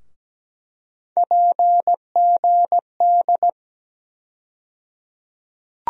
Morse code, heard as a keyed tone, about 17 wpm, 700 Hz. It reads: PGD
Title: USA Call Sign Postfixes - 3 Letters 17wpm